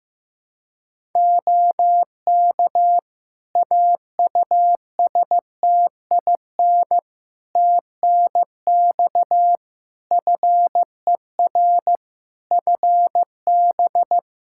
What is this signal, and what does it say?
Morse code, heard as a keyed tone, about 15 wpm, 700 Hz. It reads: OK AUSTIN TNX FER FB
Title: Practice QSO - Exchange 4 - Ending - 15wpm